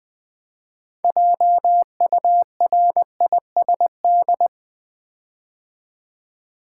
Morse code, heard as a keyed tone, about 20 wpm, 700 Hz. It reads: JURISD
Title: Build A Word - 6 Letters At A Time - 12-Letter Words - 20wpm